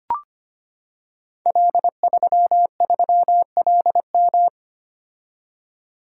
Morse code, heard as a keyed tone, about 25 wpm, 700 Hz. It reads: L33LM
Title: Code Groups - 5 Letter:Numbers - 25wpm